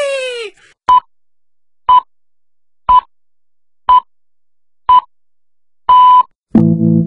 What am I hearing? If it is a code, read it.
Radio time signal (time pips). It is heard from 0.9 to 6.2 s.